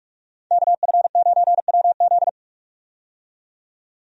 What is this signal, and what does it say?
Morse code, heard as a keyed tone, about 45 wpm, 700 Hz. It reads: KF9WZ
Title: USA Call Signs - 5 Characters - 45wpm